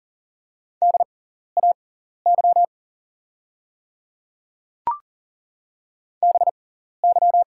Morse code, heard as a keyed tone, about 40 wpm, spaced as for 14 wpm, 700 Hz. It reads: DAY BY
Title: 6x Character Spacing - Top 100 Words - 40wpm